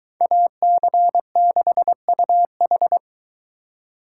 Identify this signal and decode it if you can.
Morse code, heard as a keyed tone, about 22 wpm, 700 Hz. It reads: AC6UH